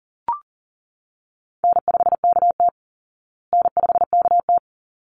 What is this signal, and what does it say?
Morse code, heard as a keyed tone, about 37 wpm, 700 Hz. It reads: N5KT N5KT
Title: Easy Call Signs Repeated 2x - 40wpm